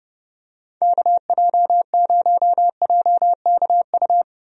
Morse code, heard as a keyed tone, about 30 wpm, 700 Hz. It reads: KJ0JKU
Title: USA Call Signs - 6 Characters - Repeated 2x - 30wpm